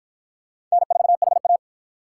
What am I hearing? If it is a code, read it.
Morse code, heard as a keyed tone, about 52 wpm, 700 Hz. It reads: N4HA